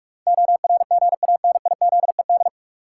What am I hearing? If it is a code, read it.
Morse code, heard as a keyed tone, about 46 wpm, 700 Hz. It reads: ORGANIZED
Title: Sentences from Top 1300 Words - New Word Drill - 45wpm